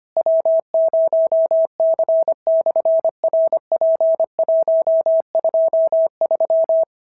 Morse code, heard as a keyed tone, about 25 wpm, 650 Hz. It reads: W0C/RP123
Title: SOTA QSO Practice 25wpm